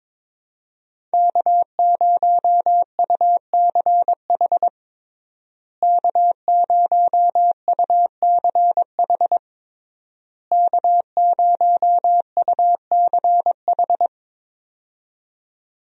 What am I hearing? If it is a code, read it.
Morse code, heard as a keyed tone, about 22 wpm, 700 Hz. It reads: K0UCH K0UCH K0UCH
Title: USA Call Signs - 5 Characters - Repeated 3x - 22wpm